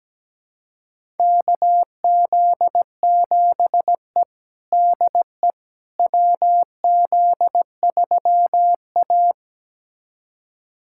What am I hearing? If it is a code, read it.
Morse code, heard as a keyed tone, about 17 wpm, 700 Hz. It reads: KZ7E DE WZ3A
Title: USA Callsign DE Callsign - 4 Characters - 17wpm